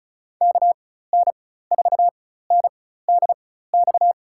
Morse code, heard as a keyed tone, about 36 wpm, spaced as for 16 wpm, 700 Hz. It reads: KN4NDX
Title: USA Call Signs - 6 Characters - 4x Character Spacing - 35wpm